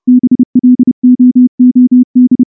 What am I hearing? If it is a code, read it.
Morse code, heard as a keyed tone, about 29 wpm, 250 Hz. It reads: BLOOD
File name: morse / morse-training-30WPM